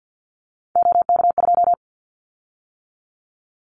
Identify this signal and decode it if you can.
Morse code, heard as a keyed tone, about 50 wpm, 690 Hz. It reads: OK2
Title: Callsign Prefixes for Top 100 DXCC - 50wpm